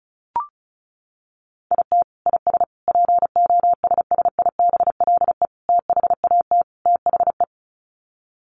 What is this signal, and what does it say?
Morse code, heard as a keyed tone, about 34 wpm, 700 Hz. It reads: IT IS POSSIBLE THAT THE
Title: Sets of 5 Words Encore2 35wpm